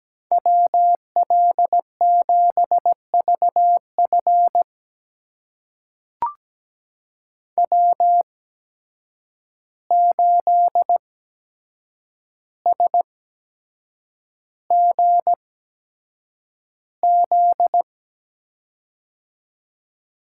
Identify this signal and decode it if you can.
Morse code, heard as a keyed tone, about 17 wpm, spaced as for 7 wpm, 700 Hz. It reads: WL7VF W8SGZ
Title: USA Call Signs - 5 Characters - 8x Character Spacing - 17wpm